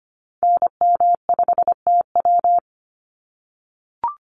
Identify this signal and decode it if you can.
Morse code, heard as a keyed tone, about 25 wpm, 690 Hz. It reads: NM5TW
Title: NA SOTA Activator Callsigns 25wpm